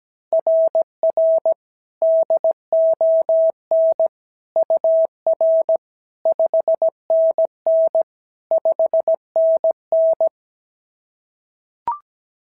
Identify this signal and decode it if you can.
Morse code, heard as a keyed tone, about 16 wpm, 650 Hz. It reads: RR DON UR 5NN 5NN
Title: SOTA QSO Practice 17wpm